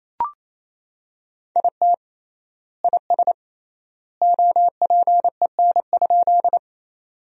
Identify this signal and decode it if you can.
Morse code, heard as a keyed tone, about 28 wpm, 700 Hz. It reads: IT IS OPEN?
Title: 3x Word-Spacing - Sentences from Top 300 Words - 28wpm